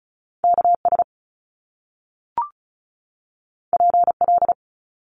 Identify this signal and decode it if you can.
Morse code, heard as a keyed tone, about 36 wpm, 700 Hz. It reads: KS PL
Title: Top Two Letter Combinations in Words 35wpm